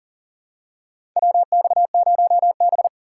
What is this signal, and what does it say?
Morse code, heard as a keyed tone, about 40 wpm, 700 Hz. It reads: WX0B
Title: USA Call Signs - 4 Characters - 4x Character Spacing - 40wpm